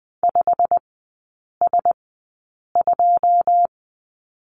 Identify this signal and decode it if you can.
Morse code, heard as a keyed tone, about 19 wpm, spaced as for 8 wpm, 690 Hz. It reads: 5S2
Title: 145 Lesson 24e - Letter y - Three Characters 20wpm